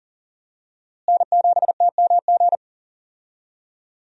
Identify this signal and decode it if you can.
Morse code, heard as a keyed tone, about 40 wpm, 700 Hz. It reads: N7TMG
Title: USA Call Signs - 5 Characters - Repeated 3x - 40wpm